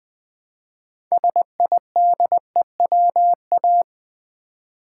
Morse code, heard as a keyed tone, about 20 wpm, 700 Hz. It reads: SIDEWA
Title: Build A Word - 3 Letters At A Time - 8-Letter Words - 20wpm